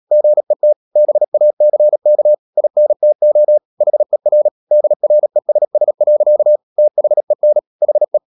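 Morse code, heard as a keyed tone, about 37 wpm, 600 Hz. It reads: GET BACK INTO HER DRESS. THEN HE
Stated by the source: Morse code practice at 35–39 wpm sending English text of Ken Follett's novel Never, from CW med Gnister.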